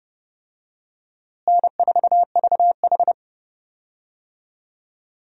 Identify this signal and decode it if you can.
Morse code, heard as a keyed tone, about 32 wpm, 700 Hz. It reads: N4VH
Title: USA Call Signs - 4 Characters - Repeated 3x - 30wpm